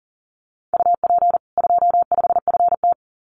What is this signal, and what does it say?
Morse code, heard as a keyed tone, about 40 wpm, 700 Hz. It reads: UP 25FT
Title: Common QSO Phrases 40wpm